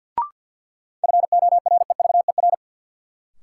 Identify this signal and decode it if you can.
Morse code, heard as a keyed tone, about 53 wpm, 700 Hz. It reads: FOREVER
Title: Sentences from Top 1600 Words - New Word Drill - 50wpm